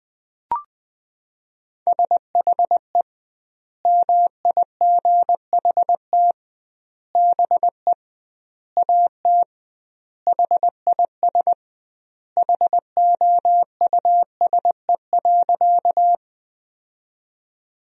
Morse code, heard as a keyed tone, about 21 wpm, 700 Hz. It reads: SHE MIGHT BE AT HIS HOUSE.
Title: 2x Word-Spacing - Sentences from Top 300 Words - 20wpm